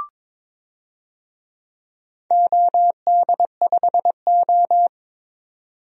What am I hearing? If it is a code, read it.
Morse code, heard as a keyed tone, about 22 wpm, 700 Hz. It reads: OD5O
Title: DXCC-100 Call Signs - 4 Characters - 22wpm